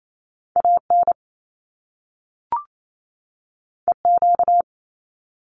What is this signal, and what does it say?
Morse code, heard as a keyed tone, about 28 wpm, 700 Hz. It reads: AN EQ
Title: Top Two Letter Combinations in Words 28wpm